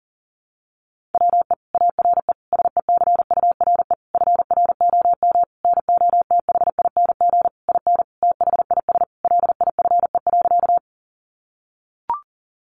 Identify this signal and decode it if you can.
Morse code, heard as a keyed tone, about 40 wpm, 700 Hz. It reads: WE ARE SECURE FROM NOTHING IN THIS LIFE.
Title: Sentences from Top 1200 Words 40wpm